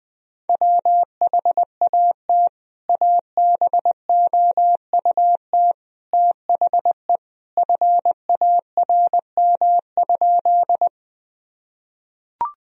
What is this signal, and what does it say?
Morse code, heard as a keyed tone, about 21 wpm, 700 Hz. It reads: WHAT ABOUT THE FARM?
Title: Sentences from Top 1000 Words - Review Part 2 - 20wpm